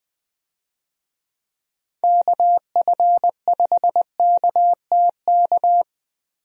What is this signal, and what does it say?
Morse code, heard as a keyed tone, about 20 wpm, 700 Hz. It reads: KF5KTK
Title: USA Call Signs - 6 Characters - Repeated 3x - 20wpm